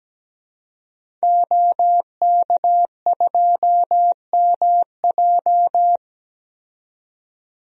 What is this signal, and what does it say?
Morse code, heard as a keyed tone, about 17 wpm, 700 Hz. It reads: OK2MJ
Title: DXCC-100 Call Signs - 5 Characters - 17wpm